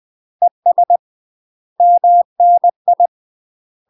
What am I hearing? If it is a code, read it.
Morse code, heard as a keyed tone, about 20 wpm, 700 Hz. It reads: ES MNI
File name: UK MASTER QSOs TRAINING / 20WPM / UK MASTER QSOs 0053